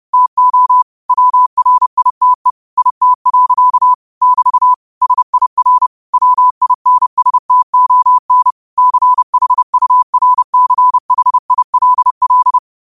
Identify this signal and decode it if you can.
Morse code, heard as a keyed tone, about 30 wpm, 1000 Hz. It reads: TO WRITE IT. <BT> SIR WINSTON CHURCHILL
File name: qotdfiles / 30wpm